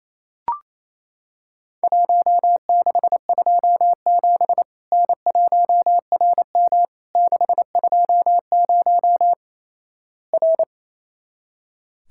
Morse code, heard as a keyed tone, about 28 wpm, 700 Hz. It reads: 1627 N1RM 620 R
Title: Worked All Europe DX Contest - QTC Exchange - 28wpm